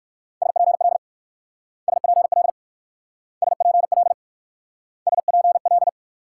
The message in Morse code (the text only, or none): SPL SPL SPL SPL